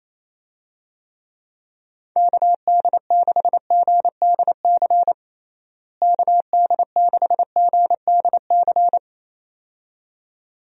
Morse code, heard as a keyed tone, about 28 wpm, 700 Hz. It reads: KD6GDC KD6GDC
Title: USA Call Signs - 6 Characters - Repeated 2x - 28wpm